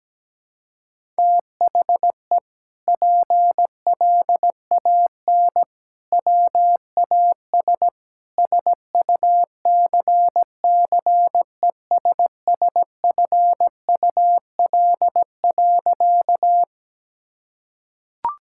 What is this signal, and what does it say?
Morse code, heard as a keyed tone, about 17 wpm, 700 Hz. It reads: THE PLAN WAS SUCCESSFUL.
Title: Sentences from Top 1000 Words - Review Part 6 - 17wpm